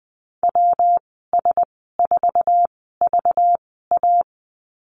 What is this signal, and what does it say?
Morse code, heard as a keyed tone, about 20 wpm, spaced as for 15 wpm, 700 Hz. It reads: WS4VA
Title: 193 Lesson 31g - Letter v - Call Signs 20wpm